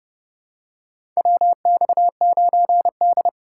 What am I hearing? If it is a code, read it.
Morse code, heard as a keyed tone, about 31 wpm, 700 Hz. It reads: WX9D
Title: USA Call Signs - 4x Character Spacing - 30wpm